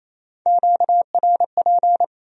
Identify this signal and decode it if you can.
Morse code, heard as a keyed tone, about 34 wpm, 700 Hz. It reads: QRP
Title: Speed-Racing Q-Codes - 28wpm